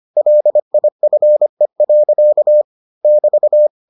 Morse code, heard as a keyed tone, about 25 wpm, 600 Hz. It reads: LIFE. <BT>